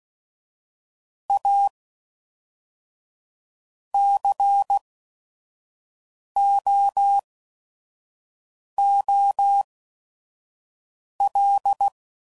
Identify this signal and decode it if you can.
Morse code, heard as a keyed tone, about 16 wpm, spaced as for 5 wpm, 800 Hz. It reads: A COOL
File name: CW Code Groups @5WPM / Weather Sets / Weather 5 1